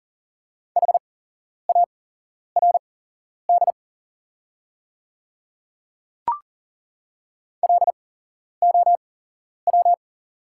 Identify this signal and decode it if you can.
Morse code, heard as a keyed tone, about 40 wpm, spaced as for 11 wpm, 700 Hz. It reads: HARD LOW
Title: 8x Character Spacing - Top 300 Words - 40wpm